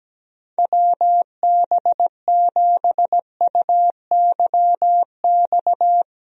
Morse code, heard as a keyed tone, about 17 wpm, 700 Hz. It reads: WB7UYX
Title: USA Call Signs - 6 Characters - Repeated 3x - 17wpm